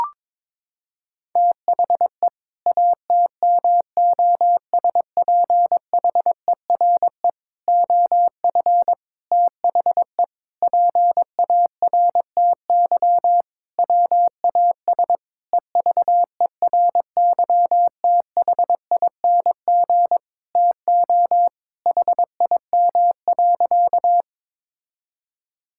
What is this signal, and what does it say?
Morse code, heard as a keyed tone, about 22 wpm, 700 Hz. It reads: THE ATMOSPHERE OF THE PARTY WAS EVERYTHING TO HIM.
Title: Sentences from Top 1600 Words 22wpm